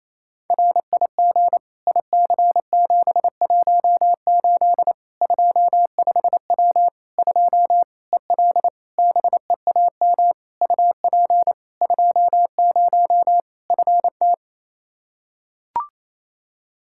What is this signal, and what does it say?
Morse code, heard as a keyed tone, about 28 wpm, 700 Hz. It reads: RIG IC718 25W 2 EL BEAM UP 20 FT
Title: Practice QSO - Exchange 2 - Info - 28wpm